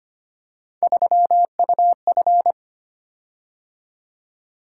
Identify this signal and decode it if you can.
Morse code, heard as a keyed tone, about 25 wpm, 700 Hz. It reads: 3UF
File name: Three Letter-Number Combinations - 25wpm